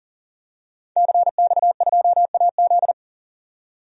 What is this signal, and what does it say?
Morse code, heard as a keyed tone, about 40 wpm, 700 Hz. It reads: CX2AZ
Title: DXCC-100 Call Signs - 5 Characters - 40wpm